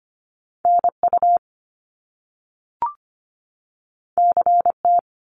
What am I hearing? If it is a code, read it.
Morse code, heard as a keyed tone, about 25 wpm, 700 Hz. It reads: NU CT
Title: Top Two Letter Combinations in Words 25wpm